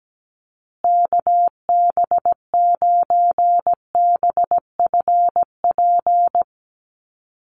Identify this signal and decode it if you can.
Morse code, heard as a keyed tone, about 17 wpm, 700 Hz. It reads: KB9BFP
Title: USA Call Signs Repeated 3x 17wpm